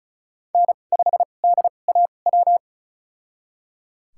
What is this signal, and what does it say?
Morse code, heard as a keyed tone, about 36 wpm, spaced as for 25 wpm, 700 Hz. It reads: N5DAW
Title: USA Call Signs - 5 Characters - 2x Character Spacing - 35wpm